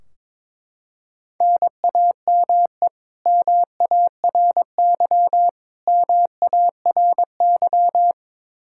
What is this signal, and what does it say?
Morse code, heard as a keyed tone, about 22 wpm, 700 Hz. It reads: NAME MARY MARY
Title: K1USN SST QSO Practice 22wpm